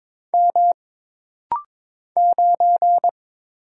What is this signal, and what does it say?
Morse code, heard as a keyed tone, about 22 wpm, 700 Hz. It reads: M 9
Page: This rapid-fire timing is geared to push your ICR (Instant Character Recognition) skills.